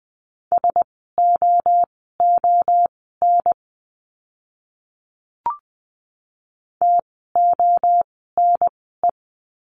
Morse code, heard as a keyed tone, about 20 wpm, spaced as for 14 wpm, 700 Hz. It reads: SOON TONE
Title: 054 Lesson 10f - Letter r - Words 20wpm